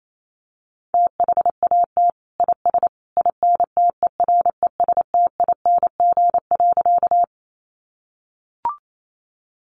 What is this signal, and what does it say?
Morse code, heard as a keyed tone, about 28 wpm, 700 Hz. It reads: THAT IS INTERESTING.
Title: Sentences from Top 1000 Words 28wpm